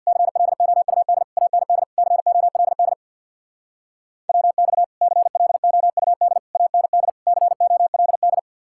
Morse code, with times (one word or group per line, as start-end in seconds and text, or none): 0.1-1.2 CLOUD
1.4-1.8 AND
2.0-2.9 COLD
4.3-4.8 WX
5.0-6.4 CLOUD
6.5-7.1 AND
7.3-8.4 COLD